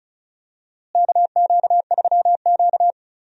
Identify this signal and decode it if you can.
Morse code, heard as a keyed tone, about 34 wpm, 700 Hz. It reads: KQ3Q
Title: USA Call Signs - 4 Characters - Repeated 3x - 35wpm